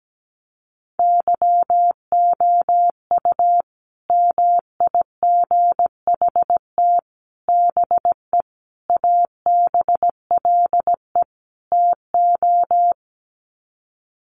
Morse code, heard as a keyed tone, about 17 wpm, 700 Hz. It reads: YOU MIGHT BE ABLE TO
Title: Sets of 5 Words Encore2 17wpm